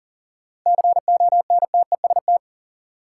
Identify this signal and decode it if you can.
Morse code, heard as a keyed tone, about 40 wpm, 700 Hz. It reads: CONTEST
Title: Sentences from Top 1600 Words - New Word Drill - 40wpm